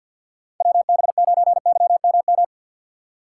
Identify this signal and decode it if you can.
Morse code, heard as a keyed tone, about 48 wpm, 700 Hz. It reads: WB9YMM
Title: USA Call Signs - 6 Characters - 4x Character Spacing - 50wpm